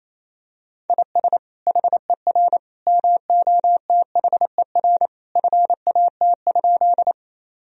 Morse code, heard as a keyed tone, about 28 wpm, 700 Hz. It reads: IS HER MOTHER FAT?
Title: Sentences from Top 1000 Words - Review Part 4 - 28wpm